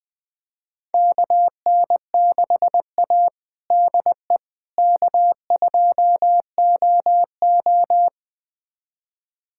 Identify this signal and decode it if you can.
Morse code, heard as a keyed tone, about 21 wpm, 700 Hz. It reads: KN6A DE K2OO
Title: USA Callsign DE Callsign - 4 Characters - 20wpm